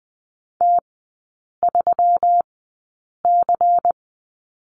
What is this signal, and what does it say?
Morse code, heard as a keyed tone, about 20 wpm, spaced as for 8 wpm, 700 Hz. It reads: T3C
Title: 121 Lesson 20e - Number 3 - Three Characters 20wpm